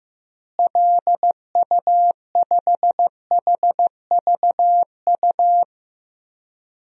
Morse code, heard as a keyed tone, about 15 wpm, 700 Hz. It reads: LU5HVU
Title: DXCC-100 Call Signs - 6 Characters - Repeated 2x - 15wpm